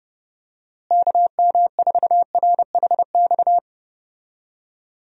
Morse code, heard as a keyed tone, about 30 wpm, 700 Hz. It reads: KM4RHX
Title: USA Call Signs - 6 Characters - Repeated 3x - 30wpm